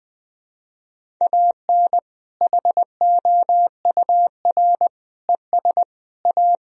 Morse code, heard as a keyed tone, about 21 wpm, 700 Hz. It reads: AN HOUR ES A
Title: Sentences from Top 400 Words and Abbreviations 20wpm